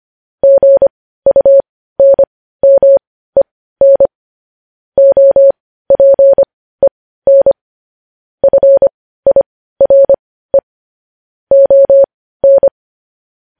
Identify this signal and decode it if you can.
Morse code, heard as a keyed tone, about 25 wpm, spaced as for 15 wpm, 550 Hz. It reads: GUNMEN OPEN FIRE ON
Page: news headlines in morse code at 15 wpm